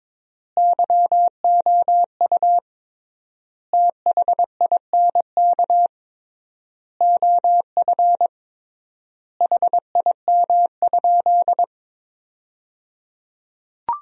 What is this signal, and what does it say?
Morse code, heard as a keyed tone, about 22 wpm, 700 Hz. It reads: YOU THINK OF HIM?
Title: 3x Word-Spacing - Sentences from Top 200 Words - 22wpm